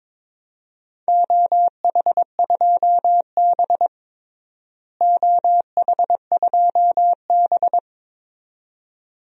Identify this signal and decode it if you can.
Morse code, heard as a keyed tone, about 22 wpm, 700 Hz. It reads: OH2B OH2B
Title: DXCC-100 Call Signs - 4 Characters - Repeated 2x - 22wpm